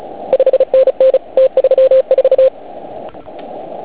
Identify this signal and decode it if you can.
Morse code, heard as a keyed tone, about 35 wpm, 530 Hz. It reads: 5NN T34
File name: Archiv / CUC149 / OK1IF CUC149